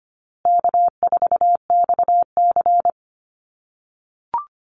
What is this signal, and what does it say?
Morse code, heard as a keyed tone, about 25 wpm, 700 Hz. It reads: K4XC
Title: USA Call Signs Repeated 3x 25wpm